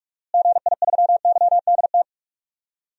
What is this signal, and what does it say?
Morse code, heard as a keyed tone, about 42 wpm, 700 Hz. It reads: GI3YDT
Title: DXCC-100 Call Signs - 6 Characters - Repeated 2x - 45wpm